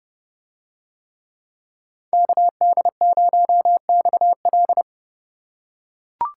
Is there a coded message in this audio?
Yes: Morse code from 2.1 to 4.8 s, heard as a keyed tone, about 28 wpm, 700 Hz. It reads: KD0XL